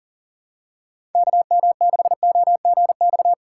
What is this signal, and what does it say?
Morse code, heard as a keyed tone, about 40 wpm, 700 Hz. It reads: KM6OGX
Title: USA Call Signs - 6 Characters - 3x Character Spacing - 40wpm